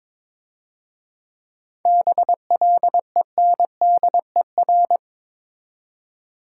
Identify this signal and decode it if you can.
Morse code, heard as a keyed tone, about 27 wpm, 690 Hz. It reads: BLENDER
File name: Easy Words - Speed-Racing with Hints - 22wpm